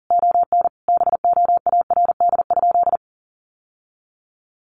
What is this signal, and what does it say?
Morse code, heard as a keyed tone, about 40 wpm, 700 Hz. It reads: ON BOARD?
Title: Sentences from Top 700 Words 40wpm